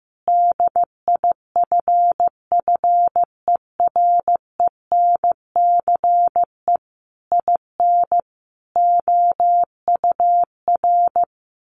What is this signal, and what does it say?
Morse code, heard as a keyed tone, about 15 wpm, 700 Hz. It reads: DIFFERENCE IN OUR
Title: Sentences from Top 1400 Words 15wpm